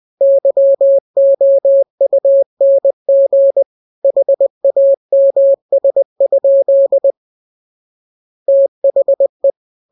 Morse code, heard as a keyed tone, about 20 wpm, 550 Hz. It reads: YOUNG HAMS? THE
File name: jokes / joke 13 20 0000